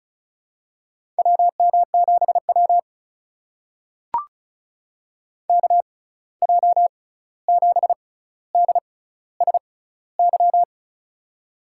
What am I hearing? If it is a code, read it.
Morse code, heard as a keyed tone, about 36 wpm, spaced as for 16 wpm, 700 Hz. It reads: WM7W KJ7DSY
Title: USA Call Signs - 6x Character Spacing - 35wpm